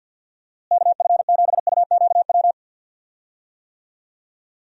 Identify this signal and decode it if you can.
Morse code, heard as a keyed tone, about 52 wpm, 700 Hz. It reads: KF7UQW
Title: USA Call Signs - 6 Characters - Repeated 2x - 50wpm